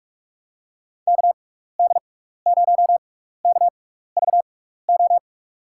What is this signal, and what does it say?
Morse code, heard as a keyed tone, about 47 wpm, spaced as for 16 wpm, 700 Hz. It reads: KD0KVO
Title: USA Call Signs - 6 Characters - 6x Character Spacing - 45wpm